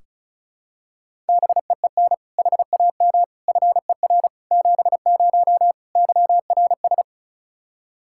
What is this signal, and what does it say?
Morse code, heard as a keyed tone, about 34 wpm, 700 Hz. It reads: BEEN HAM FER 70 YRS
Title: Practice QSO - Exchange 3 - More Info - 35wpm